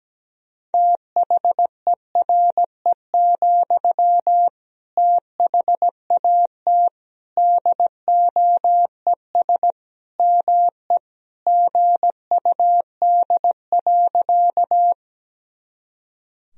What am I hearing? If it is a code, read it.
Morse code, heard as a keyed tone, about 17 wpm, 700 Hz. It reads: THERE, THAT DOES ME GUD.
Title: Sentences from Top 200 Words and Abbreviations 17wpm